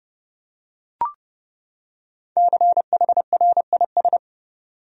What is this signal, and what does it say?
Morse code, heard as a keyed tone, about 30 wpm, 700 Hz. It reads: CHRIS
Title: USA Common Names - Max 5 Letters - 30wpm